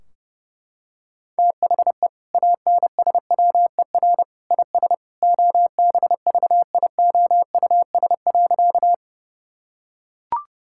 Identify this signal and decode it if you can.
Morse code, heard as a keyed tone, about 30 wpm, 700 Hz. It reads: THE ANSWER IS OBVIOUS.
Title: Sentences from Top 1600 Words 30wpm